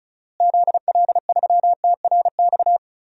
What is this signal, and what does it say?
Morse code, heard as a keyed tone, about 36 wpm, 700 Hz. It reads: ZL3TRX
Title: DXCC-100 Call Signs - 6 Characters - Repeated 3x - 35wpm